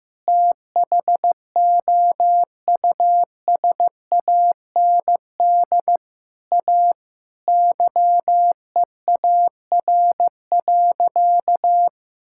Morse code, heard as a keyed tone, about 15 wpm, 700 Hz. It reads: THOUSAND A YEAR.
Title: Sentences from Top 1000 Words - Review Part 5 - 15wpm